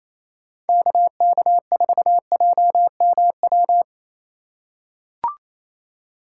Morse code, heard as a keyed tone, about 28 wpm, 700 Hz. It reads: KK4JMW
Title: USA Call Signs - 6 Characters - 4x Character Spacing - 28wpm